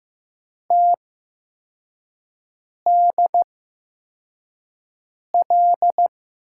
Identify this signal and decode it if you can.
Morse code, heard as a keyed tone, about 15 wpm, spaced as for 4 wpm, 700 Hz. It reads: TDL
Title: USA Call Signs - 8x Character Spacing - 15wpm